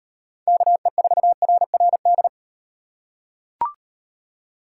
Morse code, heard as a keyed tone, about 38 wpm, 700 Hz. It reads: KE4RRD